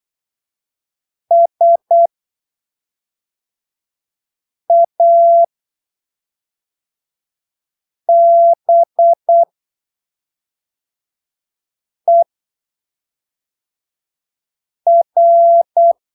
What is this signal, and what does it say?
Morse code, heard as a keyed tone, about 8 wpm, spaced as for 3 wpm, 680 Hz. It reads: SABER
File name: arquivos / texto / texto 21 08ppm